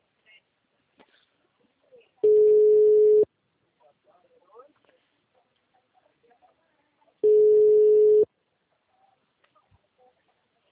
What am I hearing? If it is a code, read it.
Ringback tone (Europe, 425 Hz), with speech.